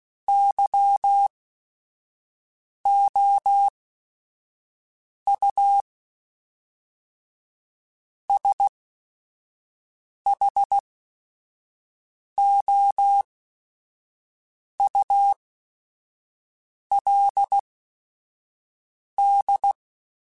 Morse code, heard as a keyed tone, about 16 wpm, spaced as for 5 wpm, 800 Hz. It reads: YOU SHOULD
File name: Chuckles / Chuckles-mp3s / Temptation @5